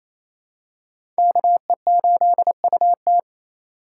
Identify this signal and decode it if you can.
Morse code, heard as a keyed tone, about 28 wpm, 700 Hz. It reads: KE8UT
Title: USA Call Signs - 5 Characters - 4x Character Spacing - 28wpm